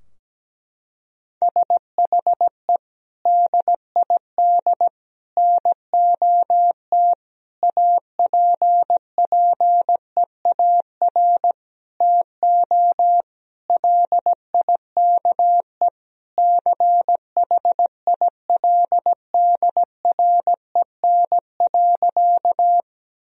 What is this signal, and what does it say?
Morse code, heard as a keyed tone, about 17 wpm, 700 Hz. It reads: SHE DID NOT APPEAR TO LIKE CHILDREN.
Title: Sentences from Top 1000 Words - Review Part 2 - 17wpm